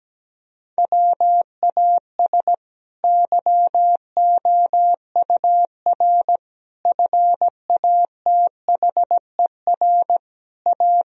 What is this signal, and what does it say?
Morse code, heard as a keyed tone, about 17 wpm, 700 Hz. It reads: WAS YOUR FATHER A
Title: Sentences from Top 1000 Words - Review Part 4 - 17wpm